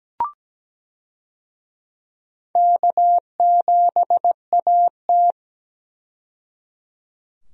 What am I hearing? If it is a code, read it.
Morse code, heard as a keyed tone, about 17 wpm, 700 Hz. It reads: K7AT
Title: USA Call Signs - 4 Characters - 17wpm